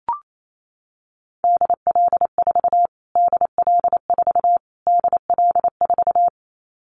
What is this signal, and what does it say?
Morse code, heard as a keyed tone, about 28 wpm, 700 Hz. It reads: DL4 DL4 DL4
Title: Callsign Prefixes for Top 25 DXCC - Repeated 3x - 28wpm